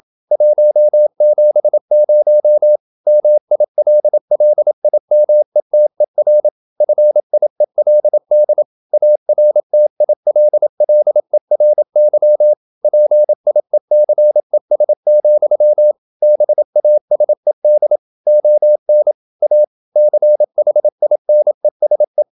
Morse code, heard as a keyed tone, about 27 wpm, 600 Hz. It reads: 170 MILLIMETER FIELD ARTILLERY PIECES, BASED ON A CHINESE
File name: Never 0027wpm